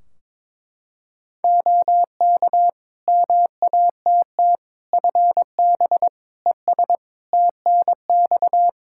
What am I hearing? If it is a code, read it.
Morse code, heard as a keyed tone, about 21 wpm, 700 Hz. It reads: OK MATT FB ES TNX
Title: Practice QSO - Exchange 2 - Info - 22wpm